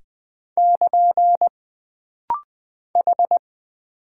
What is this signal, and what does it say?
Morse code, heard as a keyed tone, about 20 wpm, 700 Hz. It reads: <KN> H
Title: Single Character (Letter:Number:Punctuation:Prosign) - Rapid-Fire - 20wpm